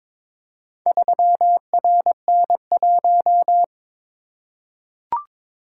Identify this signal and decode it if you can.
Morse code, heard as a keyed tone, about 22 wpm, 700 Hz. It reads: 3RN1
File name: practice / 22wpm / Four Letter-Number Combinations - 22wpm